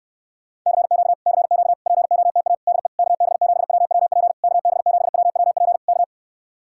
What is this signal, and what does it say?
Morse code, heard as a keyed tone, about 69 wpm, 700 Hz. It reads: CQ CQ POTA DE KB7WWP KB7WWP K